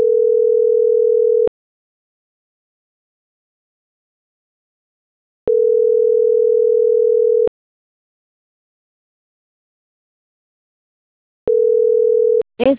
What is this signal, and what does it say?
Ringback tone (North America), 440 + 480 Hz.